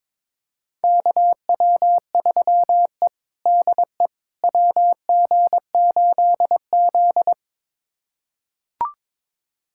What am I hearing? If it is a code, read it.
Morse code, heard as a keyed tone, about 21 wpm, 700 Hz. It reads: KW3E DE WG8Z
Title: USA Callsign DE Callsign - 4 Characters - 22wpm